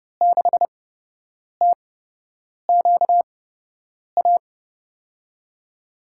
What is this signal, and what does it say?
Morse code, heard as a keyed tone, about 30 wpm, spaced as for 8 wpm, 700 Hz. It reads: 6TQA
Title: USA Call Signs - 5 Characters - 8x Character Spacing - 30wpm